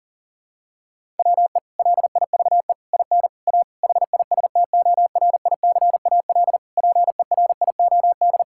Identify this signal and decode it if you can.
Morse code, heard as a keyed tone, about 37 wpm, 700 Hz. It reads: WE LIVE IN A HISTORICAL PERIOD
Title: Sentences from Top 1000 Words - Review Part 5 - 40wpm